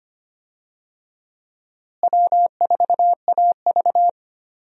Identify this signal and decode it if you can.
Morse code, heard as a keyed tone, about 25 wpm, 700 Hz. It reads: W4AV
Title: USA Call Signs - 4 Characters - Repeated 3x - 25wpm